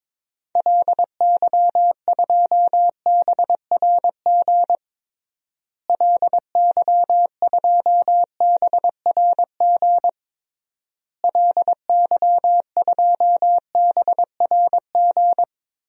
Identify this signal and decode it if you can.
Morse code, heard as a keyed tone, about 22 wpm, 700 Hz. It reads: LY2BRG LY2BRG LY2BRG
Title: DXCC-100 Call Signs - 6 Characters - Repeated 3x - 22wpm